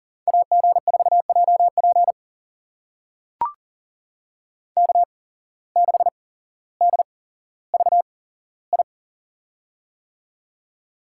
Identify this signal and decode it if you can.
Morse code, heard as a keyed tone, about 40 wpm, spaced as for 17 wpm, 700 Hz. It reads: AG4JP K6DVI
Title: USA Call Signs - 5 Characters - 8x Character Spacing - 40wpm